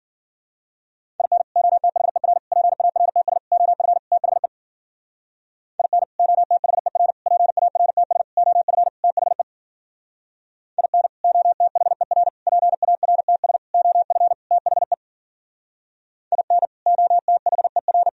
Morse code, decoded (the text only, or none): IN OTHER PARTS OF THE IN OTHER PARTS OF THE IN OTHER PARTS OF THE IN OTHER